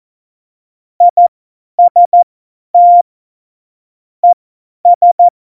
Morse code, heard as a keyed tone, about 14 wpm, spaced as for 10 wpm, 700 Hz. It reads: IST ES
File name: Morsekurs / Witze / Witz 01. Oktober 2019, 50 F70